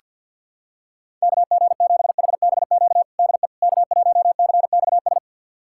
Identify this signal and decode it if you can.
Morse code, heard as a keyed tone, about 47 wpm, 700 Hz. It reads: KG7HBQ DE K1CXS
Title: USA Callsign DE Callsign 50wpm